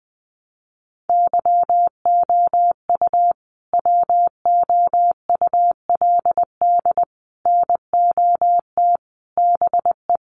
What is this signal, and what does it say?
Morse code, heard as a keyed tone, about 19 wpm, 700 Hz. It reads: YOU WOULD NOT BE
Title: Sets of 5 Words Encore2 20wpm